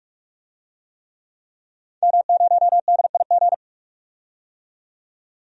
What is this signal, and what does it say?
Morse code, heard as a keyed tone, about 46 wpm, 700 Hz. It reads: M0DIG
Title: DXCC-100 Call Signs - 5 Characters - 45wpm